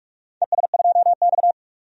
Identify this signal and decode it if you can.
Morse code, heard as a keyed tone, about 46 wpm, 700 Hz. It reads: ES2X
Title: DXCC-100 Call Signs - 4 Characters - 45wpm